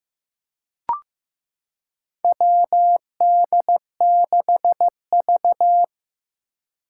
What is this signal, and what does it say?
Morse code, heard as a keyed tone, about 15 wpm, 700 Hz. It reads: WD6V
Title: Four Challenging Letter-Number Combinations 15wpm